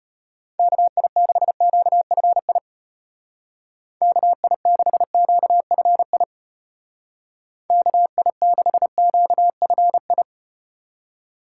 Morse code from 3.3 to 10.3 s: KI6QFI KI6QFI

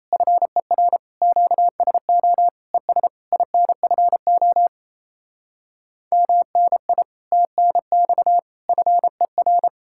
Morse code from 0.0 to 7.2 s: FER QSO ES INFO MNI

